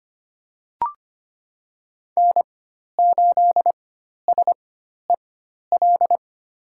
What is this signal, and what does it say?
Morse code, heard as a keyed tone, about 25 wpm, spaced as for 12 wpm, 700 Hz. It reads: N8SEL
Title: USA Call Signs - 5 Characters - 4x Character Spacing - 25wpm